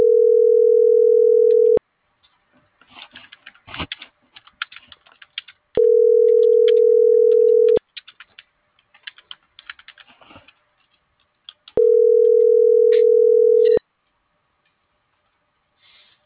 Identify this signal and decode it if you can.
Ringback tone (North America), 440 + 480 Hz.